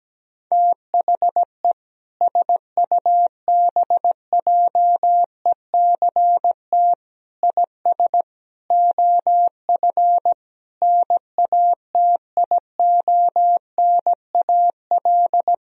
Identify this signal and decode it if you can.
Morse code, heard as a keyed tone, about 17 wpm, 700 Hz. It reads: THE SUBJECT IS OF NATIONAL
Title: Sentences from Top 1000 Words - Review Part 4 - 17wpm